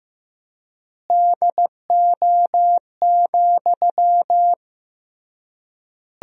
Morse code, heard as a keyed tone, about 16 wpm, 700 Hz. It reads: DO,
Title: 3x Word-Spacing - Sentences from Top 100 Words - 15wpm